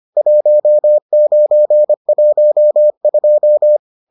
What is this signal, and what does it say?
Morse code, heard as a keyed tone, about 25 wpm, 600 Hz. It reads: 1912